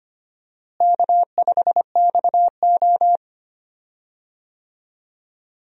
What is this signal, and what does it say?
Morse code, heard as a keyed tone, about 25 wpm, 700 Hz. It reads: K5XO
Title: USA Call Signs - 4 Characters - Repeated 3x - 25wpm